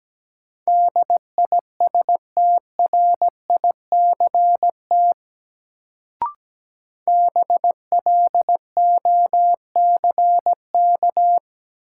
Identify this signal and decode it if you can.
Morse code, heard as a keyed tone, about 17 wpm, 700 Hz. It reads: DISTRICT BLOCK
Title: Sentences from Top 1600 Words - New Word Drill - 17wpm